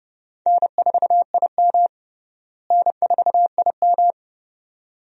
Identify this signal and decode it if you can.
Morse code, heard as a keyed tone, about 30 wpm, 700 Hz. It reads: N4IM N4IM
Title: USA Call Signs - 4 Characters - Repeated 2x - 30wpm